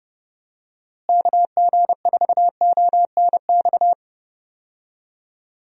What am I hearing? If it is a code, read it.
Morse code, heard as a keyed tone, about 30 wpm, 700 Hz. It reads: KG4ONX
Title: USA Call Signs - 6 Characters - Repeated 2x - 30wpm